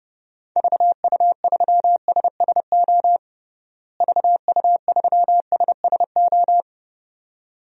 Morse code, heard as a keyed tone, about 30 wpm, 700 Hz. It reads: VU3SSO VU3SSO
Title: DXCC-100 Call Signs - 6 Characters - Repeated 2x - 30wpm